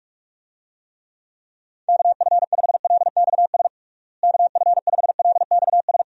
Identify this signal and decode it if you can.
Morse code, heard as a keyed tone, about 46 wpm, 700 Hz. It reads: KF5LXS KF5LXS
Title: USA Call Signs - 6 Characters - Repeated 2x - 45wpm